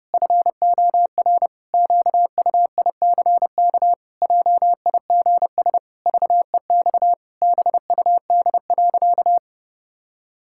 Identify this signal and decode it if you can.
Morse code, heard as a keyed tone, about 30 wpm, 700 Hz. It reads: FOR QUICK JIGS VEX BUD.
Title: Pangram Challenge v2 - 30wpm